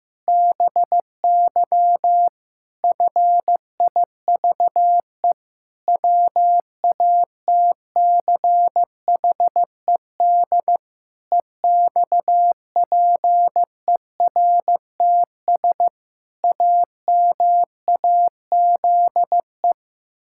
Morse code, heard as a keyed tone, about 15 wpm, 700 Hz. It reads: BY FIVE WATCHED EXPERTS AMAZE
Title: Pangram Challenge v2 - 15wpm